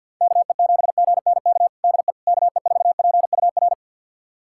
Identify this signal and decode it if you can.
Morse code, heard as a keyed tone, about 46 wpm, 700 Hz. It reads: KE7GNK DE KE4PUR
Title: USA Callsign DE Callsign - 6 Characters - 50wpm